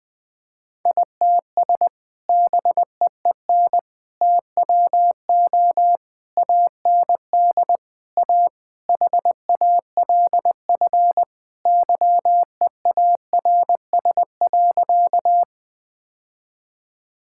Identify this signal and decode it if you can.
Morse code, heard as a keyed tone, about 20 wpm, 700 Hz. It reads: ITS BEEN TWO AND A HALF YEARS.
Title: Sentences with 5-Word Phrases - 20wpm